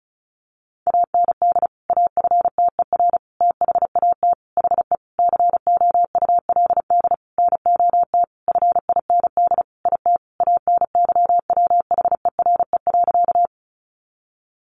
Morse code, heard as a keyed tone, about 34 wpm, 700 Hz. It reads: AND AFTER THAT HE COULD NOT FIND IT ANYWHERE.
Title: Sentences from Top 1400 Words 35wpm